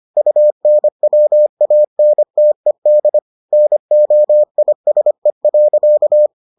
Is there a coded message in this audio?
Yes: Morse code from 0.2 to 6.3 s, heard as a keyed tone, about 25 wpm, 600 Hz. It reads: UNWANTED NOISE.